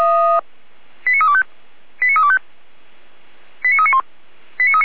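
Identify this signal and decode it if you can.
Five-tone selective call (ZVEI), heard as quick runs of five tones, tones of about 70 ms. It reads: 89325 89325 89391
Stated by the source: Received on an SDR as a ZVEI five-tone call, address 89315.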